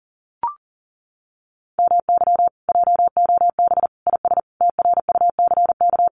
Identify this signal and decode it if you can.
Morse code, heard as a keyed tone, about 40 wpm, 700 Hz. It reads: MY JOB IS TRUCK
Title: Common QSO Phrases Encore 40wpm